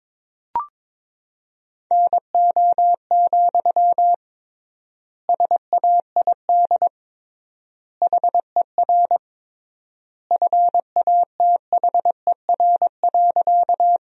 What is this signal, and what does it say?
Morse code, heard as a keyed tone, about 22 wpm, 700 Hz. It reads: NO, SAID HER FATHER.
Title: 3x Word-Spacing - Sentences from Top 300 Words - 22wpm